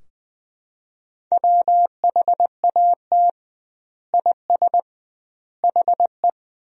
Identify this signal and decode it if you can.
Morse code, heard as a keyed tone, about 21 wpm, 700 Hz. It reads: WHAT IS HE
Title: 2x Word-Spacing - Sentences from Top 100 Words - 20wpm